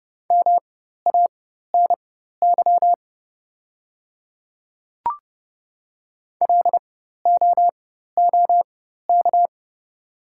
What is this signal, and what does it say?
Morse code, heard as a keyed tone, about 30 wpm, spaced as for 14 wpm, 700 Hz. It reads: MANY LOOK
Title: 4x Character Spacing - Top 100 Words - 30wpm